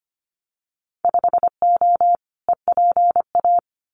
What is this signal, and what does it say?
Morse code, heard as a keyed tone, about 25 wpm, 700 Hz. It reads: 5O EPA
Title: Winter Field Day Exchanges Encore 25wpm